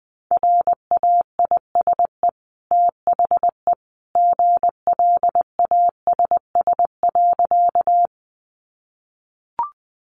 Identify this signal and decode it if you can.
Morse code, heard as a keyed tone, about 20 wpm, 700 Hz. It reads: RAISE THE GLASS.
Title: Sentences from Top 1400 Words 20wpm